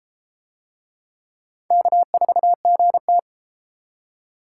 Morse code, heard as a keyed tone, about 30 wpm, 700 Hz. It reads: K4GT